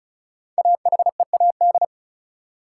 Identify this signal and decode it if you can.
Morse code, heard as a keyed tone, about 36 wpm, 700 Hz. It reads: AHEAD
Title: Sentences from Top 1000 Words - Review Part 6 - New Word Drill - 35wpm